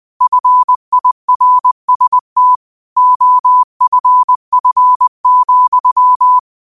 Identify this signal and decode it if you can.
Morse code, heard as a keyed tone, about 20 wpm, 1000 Hz. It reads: FIRST OFF,
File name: qotdfiles / 20wpm